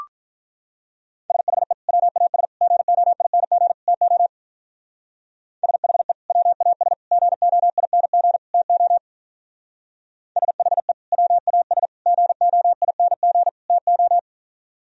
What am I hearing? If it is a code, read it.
Morse code, heard as a keyed tone, about 43 wpm, 700 Hz. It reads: SHE WAS GOING TO SHE WAS GOING TO SHE WAS GOING TO
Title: Speed-Racing - Sets of 4 Words 35wpm